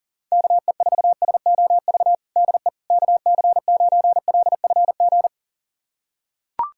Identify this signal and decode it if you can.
Morse code, heard as a keyed tone, about 37 wpm, 700 Hz. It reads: KE4SOV DE KC9LFG
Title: USA Callsign DE Callsign - 6 Characters - 40wpm